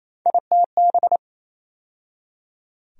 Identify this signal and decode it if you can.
Morse code, heard as a keyed tone, about 28 wpm, 700 Hz. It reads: ITB
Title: USA Call Sign Postfixes - 3 Letters 28wpm